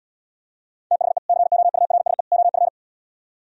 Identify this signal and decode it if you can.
Morse code, heard as a keyed tone, about 77 wpm, 700 Hz. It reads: THE COURSE OF